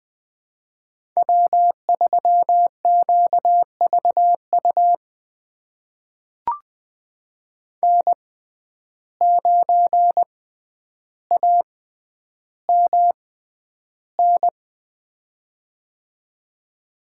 Morse code, heard as a keyed tone, about 20 wpm, spaced as for 10 wpm, 700 Hz. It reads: W3QVU N9AMN
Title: USA Call Signs - 5 Characters - 6x Character Spacing - 20wpm